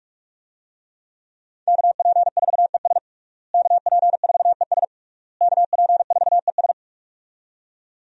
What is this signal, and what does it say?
Morse code, heard as a keyed tone, about 42 wpm, 700 Hz. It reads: KP4ES KP4ES KP4ES
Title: USA Call Signs - 5 Characters - Repeated 3x - 45wpm